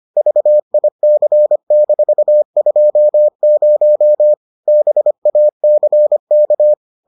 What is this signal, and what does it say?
Morse code, heard as a keyed tone, about 25 wpm, 600 Hz. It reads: VIC-20 BACK